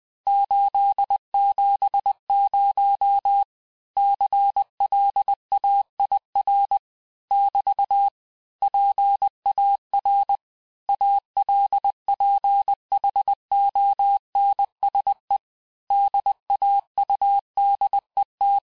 Morse code, heard as a keyed tone, about 20 wpm, 800 Hz. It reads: 870 CLAIR <BT> PAR ALPHONSE DAUDET
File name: cours-du-mercredi / audios / cours-870-texte-CLAIR-vitesse-19-MPM